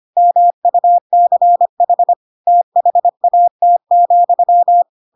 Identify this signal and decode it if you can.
Morse code, heard as a keyed tone, about 25 wpm, 700 Hz. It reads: MUCH THAT,
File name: Pinocchio / Pinocchio ch3 25 25 ik1hge